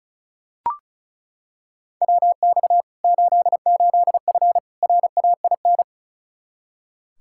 Morse code, heard as a keyed tone, about 36 wpm, 700 Hz. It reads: WX 88F RAIN
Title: Practice QSO - Exchange 2 - Info - 35wpm